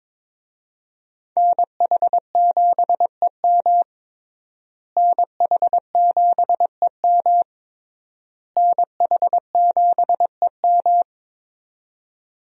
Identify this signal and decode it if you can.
Morse code, heard as a keyed tone, about 22 wpm, 700 Hz. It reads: NH7EM NH7EM NH7EM
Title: USA Call Signs - 5 Characters - Repeated 3x - 22wpm